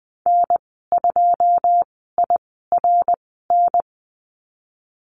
Morse code, heard as a keyed tone, about 20 wpm, spaced as for 14 wpm, 700 Hz. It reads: N2IRN
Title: 161 Lesson 26g - Letter g - Call Signs 20wpm